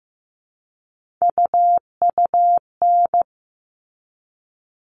Morse code, heard as a keyed tone, about 15 wpm, 700 Hz. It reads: UUN
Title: All Cut Numbers - 3-Digit 15wpm